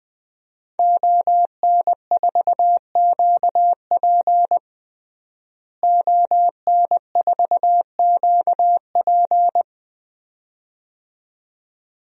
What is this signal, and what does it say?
Morse code, heard as a keyed tone, about 21 wpm, 700 Hz. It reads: ON4QP ON4QP
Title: DXCC-100 Call Signs - 5 Characters - Repeated 2x - 20wpm